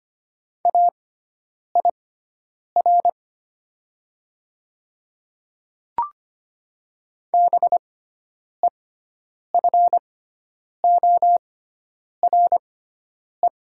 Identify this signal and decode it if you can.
Morse code, heard as a keyed tone, about 25 wpm, spaced as for 9 wpm, 700 Hz. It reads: AIR BEFORE
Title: 6x Character Spacing - Top 200 Words - 25wpm